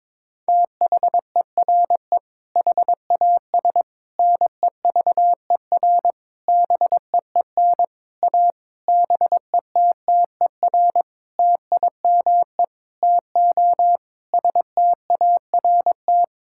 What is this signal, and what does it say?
Morse code, heard as a keyed tone, about 22 wpm, 700 Hz. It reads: THERE HAS NEVER BEEN A BETTER TIME TO START
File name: Sentences with 7-Word Phrases - 22wpm